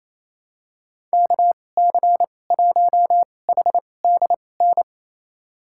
Morse code, heard as a keyed tone, about 28 wpm, spaced as for 20 wpm, 700 Hz. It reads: KC1HDN
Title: USA Call Signs - 2x Character Spacing - 28wpm